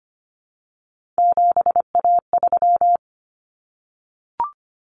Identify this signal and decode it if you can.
Morse code, heard as a keyed tone, about 25 wpm, 700 Hz. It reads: 7A3
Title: Common Cut Numbers - 3-Digit 25wpm